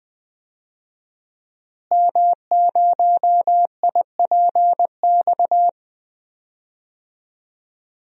Morse code, heard as a keyed tone, about 20 wpm, 700 Hz. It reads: M0IPX
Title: DXCC-100 Call Signs - 5 Characters - Repeated 2x - 20wpm